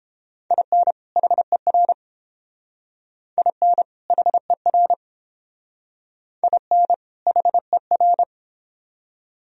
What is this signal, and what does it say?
Morse code, heard as a keyed tone, about 29 wpm, 700 Hz. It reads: IN HER IN HER IN HER